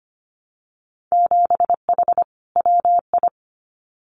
Morse code, heard as a keyed tone, about 25 wpm, 700 Hz. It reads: 7H WI
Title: Winter Field Day Exchanges Encore 25wpm